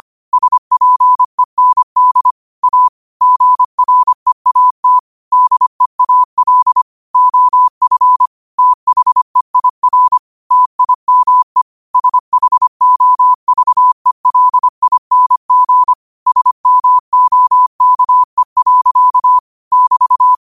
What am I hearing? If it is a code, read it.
Morse code, heard as a keyed tone, about 25 wpm, 1000 Hz. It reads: SPEND A GREAT DEAL OF THEIR TIME SHOVELING SMOKE. <BT>
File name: 25wpm